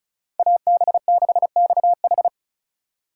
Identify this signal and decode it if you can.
Morse code, heard as a keyed tone, about 36 wpm, 700 Hz. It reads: AB6XH
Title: USA Call Signs - 3x Character Spacing - 35wpm